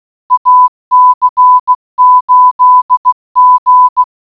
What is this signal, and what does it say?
Morse code, heard as a keyed tone, about 16 wpm, 1000 Hz. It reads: AC8G